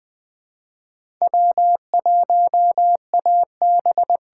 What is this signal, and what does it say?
Morse code, heard as a keyed tone, about 20 wpm, 700 Hz. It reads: W1AB
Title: USA Call Signs - 6x Character Spacing - 20wpm